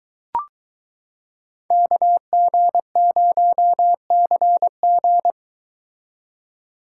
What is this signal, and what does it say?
Morse code, heard as a keyed tone, about 23 wpm, 700 Hz. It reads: KG0CG